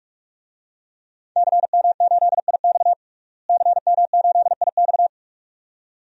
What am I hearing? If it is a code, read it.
Morse code, heard as a keyed tone, about 41 wpm, 700 Hz. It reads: CM8IX CM8IX
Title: DXCC-100 Call Signs - 5 Characters - Repeated 2x - 45wpm